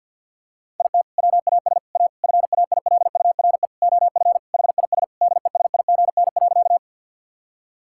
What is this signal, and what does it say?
Morse code, heard as a keyed tone, about 50 wpm, 700 Hz. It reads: IT WAS A FAILURE OF HIS DESIGN.
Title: Sentences from Top 1000 Words - Review Part 6 - 50wpm